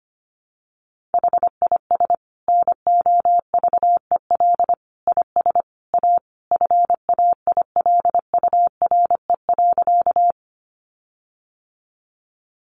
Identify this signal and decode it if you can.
Morse code, heard as a keyed tone, about 25 wpm, 700 Hz. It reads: HIS NOVEL IS A FAILURE.
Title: Sentences from Top 1300 Words 25wpm